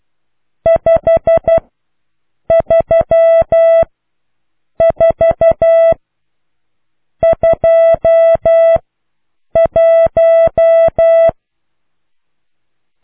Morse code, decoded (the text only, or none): 53421